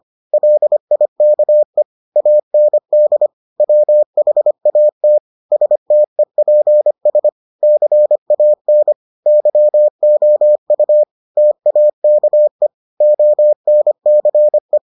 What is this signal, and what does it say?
Morse code, heard as a keyed tone, about 25 wpm, 600 Hz. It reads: LIKE AND WHAT STEPS CAN YOU TAKE ONCE